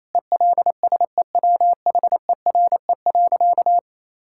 Morse code, heard as a keyed tone, about 28 wpm, 700 Hz. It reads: ELSEWHERE.
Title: Sentences from Top 1600 Words 28wpm